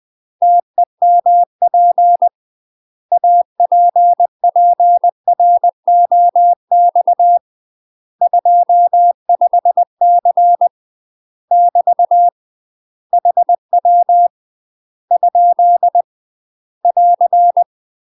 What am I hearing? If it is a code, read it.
Morse code, heard as a keyed tone, about 20 wpm, 700 Hz. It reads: TEMP APPROX 25C <BT> HW ? <AR>
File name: UK MASTER QSOs TRAINING/20WPM